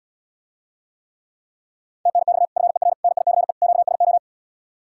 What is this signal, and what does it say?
Morse code, heard as a keyed tone, about 80 wpm, 700 Hz. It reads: TNX FER NICE QSO